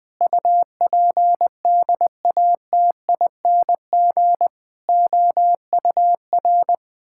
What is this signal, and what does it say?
Morse code, heard as a keyed tone, about 20 wpm, 700 Hz. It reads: UPDATING OUR
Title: Sentences with 6-Word Phrases - 20wpm